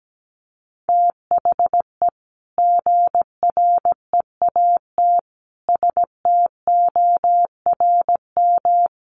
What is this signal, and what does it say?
Morse code, heard as a keyed tone, about 17 wpm, 700 Hz. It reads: THE GREAT STORM
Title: Sentences from Top 1500 Words 17wpm